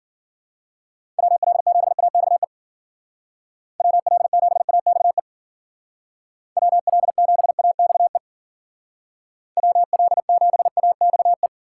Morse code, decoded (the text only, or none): WL7AXE WL7AXE WL7AXE WL7AXE